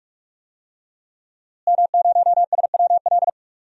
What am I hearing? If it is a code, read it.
Morse code, heard as a keyed tone, about 41 wpm, 700 Hz. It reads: M0SWL
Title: DXCC-100 Call Signs - 5 Characters - Repeated 3x - 45wpm